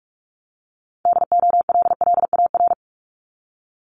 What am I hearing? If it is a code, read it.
Morse code, heard as a keyed tone, about 46 wpm, 690 Hz. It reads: DOLLAR
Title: Multi-letter Phonemes Repeated 3x 45wpm